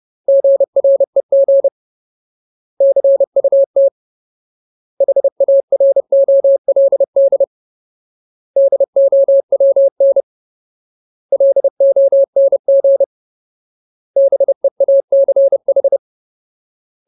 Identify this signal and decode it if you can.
Morse code, heard as a keyed tone, about 30 wpm, 550 Hz. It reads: GREG CUT HAROLD DOWN LONG BEACH
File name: Ultimate Word List / 30WPM Ultimate